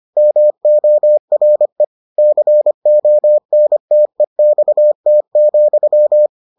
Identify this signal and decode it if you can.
Morse code, heard as a keyed tone, about 25 wpm, 600 Hz. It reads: MORE CONTEXT,